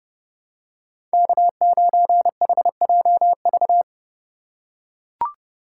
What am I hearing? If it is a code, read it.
Morse code, heard as a keyed tone, about 28 wpm, 700 Hz. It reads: K9HJV